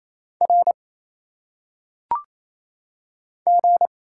Morse code, heard as a keyed tone, about 27 wpm, 700 Hz. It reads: R G
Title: Single Letters - Phonetic Pronunciation 28wpm